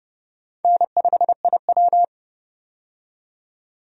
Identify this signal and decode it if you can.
Morse code, heard as a keyed tone, about 30 wpm, 700 Hz. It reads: N5IW
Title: USA Call Signs - 4 Characters - 30wpm